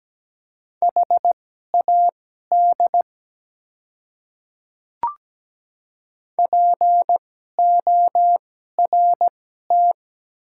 Morse code, heard as a keyed tone, about 17 wpm, spaced as for 12 wpm, 700 Hz. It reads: HAD PORT
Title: 2x Character Spacing - Top 200 Words - 17wpm